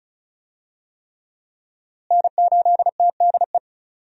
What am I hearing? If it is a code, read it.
Morse code, heard as a keyed tone, about 34 wpm, 700 Hz. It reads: N8TDE
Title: USA Call Signs - 5 Characters - Repeated 3x - 35wpm